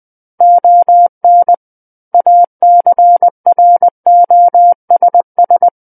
Morse code, heard as a keyed tone, about 20 wpm, 700 Hz. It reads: ON ACROSS
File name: 20 WPM